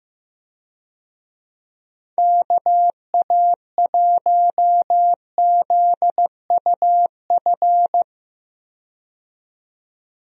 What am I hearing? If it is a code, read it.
Morse code, heard as a keyed tone, about 15 wpm, 700 Hz. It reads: KA1ZUF
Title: USA Call Signs - 6 Characters - 15wpm